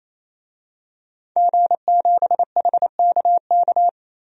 Morse code, heard as a keyed tone, about 28 wpm, 700 Hz. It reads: G7HKK
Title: DXCC-100 Call Signs - 5 Characters - 28wpm